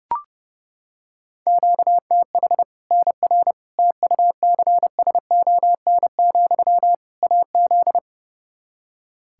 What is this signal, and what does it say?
Morse code, heard as a keyed tone, about 30 wpm, 700 Hz. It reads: QTH NR TUCSON, AZ
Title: K1USN SST QSO Practice 30wpm